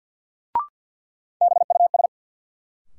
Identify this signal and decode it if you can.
Morse code, heard as a keyed tone, about 53 wpm, 700 Hz. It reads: BUS
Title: Sentences from Top 1000 Words - Review Part 6 - New Word Drill - 50wpm